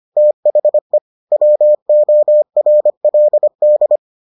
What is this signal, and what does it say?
Morse code, heard as a keyed tone, about 25 wpm, 600 Hz. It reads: THE WORLD